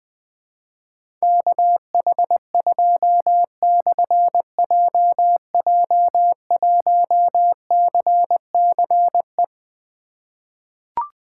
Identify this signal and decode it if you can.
Morse code, heard as a keyed tone, about 21 wpm, 700 Hz. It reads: KH2/JJ1CCE
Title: Difficult Call Signs 20wpm